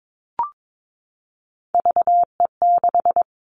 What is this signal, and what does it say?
Morse code, heard as a keyed tone, about 22 wpm, 700 Hz. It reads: VE6
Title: Callsign Prefixes for Top 50 DXCC - 22wpm